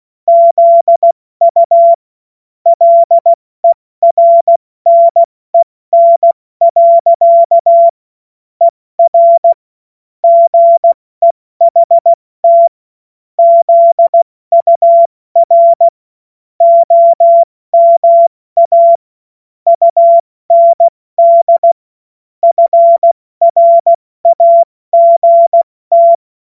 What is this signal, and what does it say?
Morse code, heard as a keyed tone, about 16 wpm, 680 Hz. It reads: ZU LERNEN. ER GEHT ZUR OMA UND FRAGT
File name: Morsekurs / Witze / Witz 06. Juli 2021, 70 F80